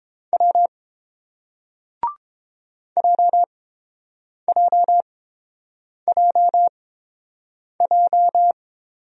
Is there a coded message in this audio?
Yes: Morse code from 0.3 to 8.5 s, heard as a keyed tone, about 30 wpm, spaced as for 7 wpm, 700 Hz. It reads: W JJJJ